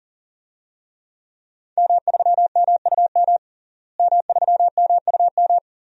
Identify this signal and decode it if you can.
Morse code, heard as a keyed tone, about 40 wpm, 700 Hz. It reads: M3MUM M3MUM
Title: DXCC-100 Call Signs - 5 Characters - Repeated 2x - 40wpm